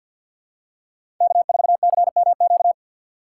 Morse code, heard as a keyed tone, about 47 wpm, 700 Hz. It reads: K4CMQ
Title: USA Call Signs - 5 Characters - 2x Character Spacing - 50wpm